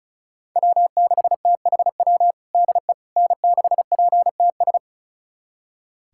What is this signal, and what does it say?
Morse code, heard as a keyed tone, about 34 wpm, 700 Hz. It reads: W6THW DE N6PTS
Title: USA Callsign DE Callsign - 5 Characters - 35wpm